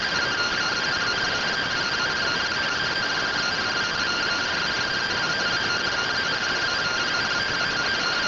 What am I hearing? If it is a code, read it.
Two-tone teleprinter signal, about 170 Hz shift.